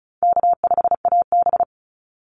Morse code, heard as a keyed tone, about 34 wpm, 700 Hz. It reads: K5AB
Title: Easy Call Signs Repeated 2x - 35wpm